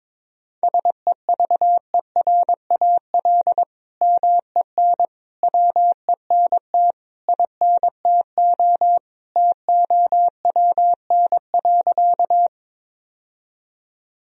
Morse code, heard as a keyed tone, about 21 wpm, 700 Hz. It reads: SEVERAL MEN WENT INTO TOWN.
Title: Sentences from Top 1000 Words - Review Part 2 - 22wpm